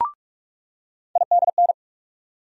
Morse code, heard as a keyed tone, about 42 wpm, 700 Hz. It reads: IDN
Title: USA Call Sign Postfixes - 3 Letters 45wpm